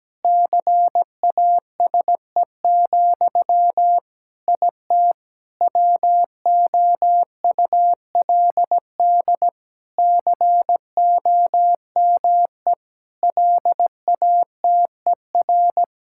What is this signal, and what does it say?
Morse code, heard as a keyed tone, about 17 wpm, 700 Hz. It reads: CASE, IT WOULD COME LATER
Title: Sentences from Top 1600 Words 17wpm